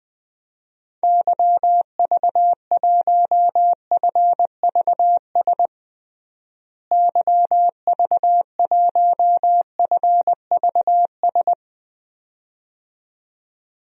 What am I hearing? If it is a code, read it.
Morse code, heard as a keyed tone, about 20 wpm, 700 Hz. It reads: YV1FVS YV1FVS
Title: DXCC-100 Call Signs - 6 Characters - Repeated 2x - 20wpm